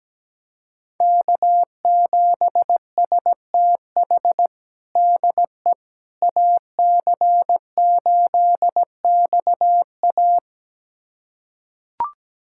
Morse code, heard as a keyed tone, about 17 wpm, 700 Hz. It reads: K7STH DE AC8XA
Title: USA Callsign DE Callsign - 5 Characters - 17wpm